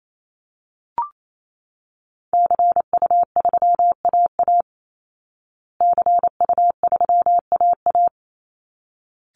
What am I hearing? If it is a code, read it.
Morse code, heard as a keyed tone, about 28 wpm, 700 Hz. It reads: CU3AA CU3AA
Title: Call Signs Repeated 3x - 28wpm